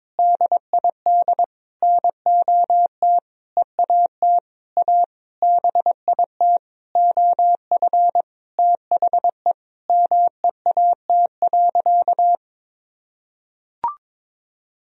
Morse code, heard as a keyed tone, about 22 wpm, 700 Hz. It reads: DID NOT EAT A BIT OF THE MEAT.
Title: Sentences from Top 1000 Words - Review Part 3 - 22wpm